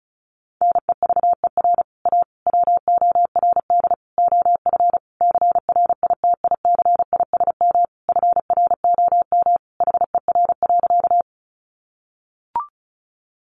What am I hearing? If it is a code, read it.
Morse code, heard as a keyed tone, about 34 wpm, 700 Hz. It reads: NEVER A WORD OF CRITICISM FROM HER.
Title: Sentences from Top 1300 Words 35wpm